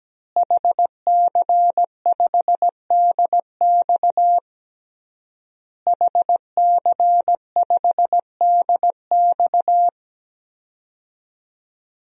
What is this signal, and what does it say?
Morse code, heard as a keyed tone, about 17 wpm, 700 Hz. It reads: HC5DX HC5DX
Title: DXCC-100 Call Signs - 5 Characters - Repeated 3x - 17wpm